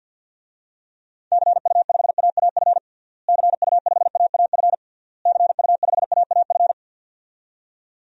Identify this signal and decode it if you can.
Morse code, heard as a keyed tone, about 48 wpm, 700 Hz. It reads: CU5AAF CU5AAF CU5AAF
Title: DXCC-100 Call Signs - 6 Characters - Repeated 3x - 50wpm